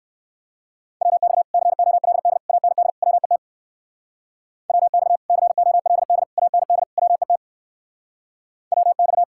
Morse code, heard as a keyed tone, about 64 wpm, 700 Hz. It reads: WX COLD AND WET WX COLD AND WET WX